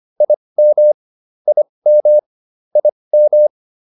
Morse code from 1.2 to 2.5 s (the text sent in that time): IM